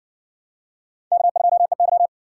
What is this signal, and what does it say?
Morse code, heard as a keyed tone, about 56 wpm, 700 Hz. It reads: K2EY